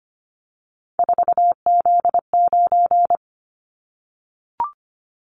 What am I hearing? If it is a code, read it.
Morse code, heard as a keyed tone, about 25 wpm, 700 Hz. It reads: 4Z9
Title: Callsign Prefixes for Top 50 DXCC - 25wpm